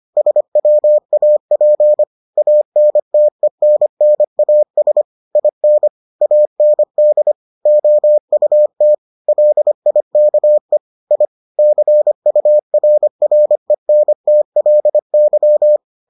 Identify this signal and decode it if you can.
Morse code, heard as a keyed tone, about 25 wpm, 600 Hz. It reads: SWAP ANTENNAS IN AND OUT LIKE I CURRENTLY